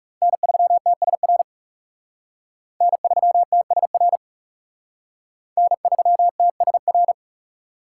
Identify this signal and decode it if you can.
Morse code, heard as a keyed tone, about 35 wpm, 700 Hz. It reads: N3TSR N3TSR N3TSR
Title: Speed Racing - USA Call Signs 30wpm